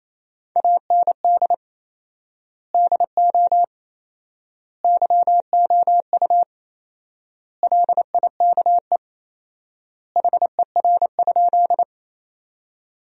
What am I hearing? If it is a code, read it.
Morse code, heard as a keyed tone, about 28 wpm, 700 Hz. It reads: AND DO YOU LIKE HER?
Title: 4x Word-Spacing - Sentences from Top 100 Words - 28wpm